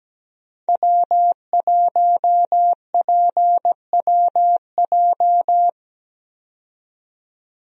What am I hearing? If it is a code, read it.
Morse code, heard as a keyed tone, about 17 wpm, 700 Hz. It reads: W1PWJ
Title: USA Call Signs - 5 Characters - 17wpm